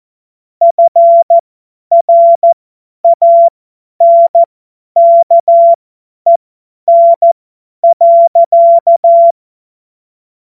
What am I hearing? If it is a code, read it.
Morse code, heard as a keyed tone, about 14 wpm, spaced as for 10 wpm, 680 Hz. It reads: FRANKEN.
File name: Morsekurs / Witze / Witz 19. April 2022 50 F70